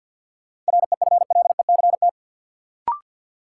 Morse code, heard as a keyed tone, about 53 wpm, 700 Hz. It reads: REFLECT
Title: Sentences from Top 1500 Words - New Word Drill - 50wpm